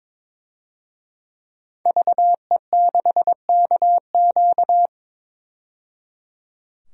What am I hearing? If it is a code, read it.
Morse code, heard as a keyed tone, about 22 wpm, 700 Hz. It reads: VE6KQ
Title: DXCC-100 Call Signs - 5 Characters - 22wpm